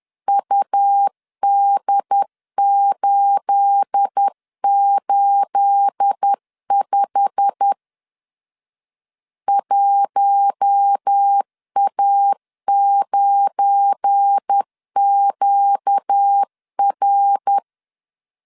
Morse code, heard as a keyed tone, about 10 wpm, 800 Hz. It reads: UD885 1A9QR
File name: CW / DET / 1 9 1500C S A 10wpm 4511